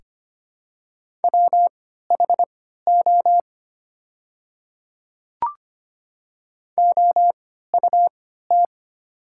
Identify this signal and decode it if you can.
Morse code, heard as a keyed tone, about 25 wpm, spaced as for 14 wpm, 700 Hz. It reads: WHO OUT
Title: 3x Character Spacing - Top 300 Words - 25wpm